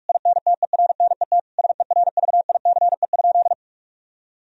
Morse code, heard as a keyed tone, about 48 wpm, 700 Hz. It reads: INTERNET SERVICE?
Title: Sentences from Top 1000 Words - Review Part 4 - 45wpm